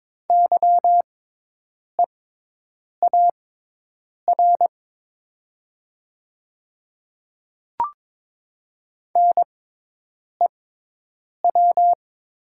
Morse code, heard as a keyed tone, about 22 wpm, spaced as for 8 wpm, 700 Hz. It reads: YEAR NEW
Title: 6x Character Spacing - Top 200 Words - 22wpm